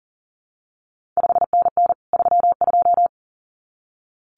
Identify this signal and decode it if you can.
Morse code, heard as a keyed tone, about 39 wpm, 700 Hz. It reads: 5NN 32
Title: CQ WW DX Contest Exchange Encore - 40wpm